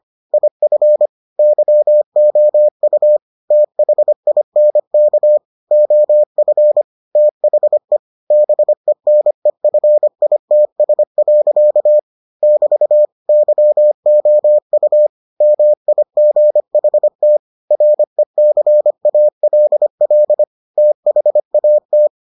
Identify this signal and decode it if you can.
Morse code, heard as a keyed tone, about 25 wpm, 600 Hz. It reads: IF YOU THINK OF THE BENEFITS. <BT> YOU MIGHT RECALL THAT